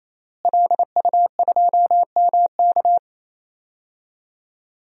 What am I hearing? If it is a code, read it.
Morse code, heard as a keyed tone, about 28 wpm, 700 Hz. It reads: LU2MK
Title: DXCC-100 Call Signs - 5 Characters - Repeated 3x - 28wpm